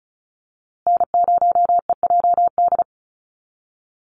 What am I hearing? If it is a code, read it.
Morse code, heard as a keyed tone, about 34 wpm, 700 Hz. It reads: N0EJD
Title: USA Call Signs Repeated 2x 35wpm